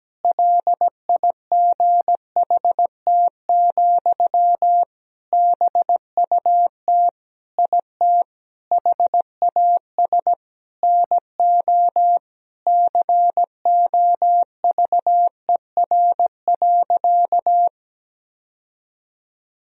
Morse code, heard as a keyed tone, about 17 wpm, 700 Hz. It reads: LIGHT, BUT IT HAS NO COVER.
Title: Sentences from Top 1000 Words - Review Part 1 - 17wpm